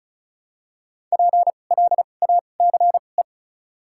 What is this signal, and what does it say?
Morse code, heard as a keyed tone, about 34 wpm, spaced as for 25 wpm, 700 Hz. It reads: PLACE
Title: 2x Character Spacing - Top 500 Words - 35wpm